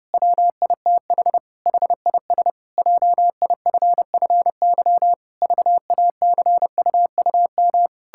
Morse code, heard as a keyed tone, about 30 wpm, 700 Hz. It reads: WITH HIS JIFFY VACUUM
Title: Pangram Challenge v2 - 30wpm